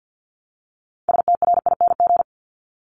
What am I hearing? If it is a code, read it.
Morse code, heard as a keyed tone, about 50 wpm, 700 Hz. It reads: STRING
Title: Multi-letter Phonemes Repeated 3x 50wpm